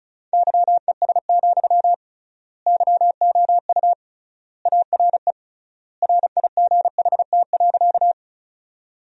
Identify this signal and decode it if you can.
Morse code, heard as a keyed tone, about 36 wpm, 700 Hz. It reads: YES, YOU ARE RIGHT.
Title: 3x Word-Spacing - Sentences from Top 500 Words - 35wpm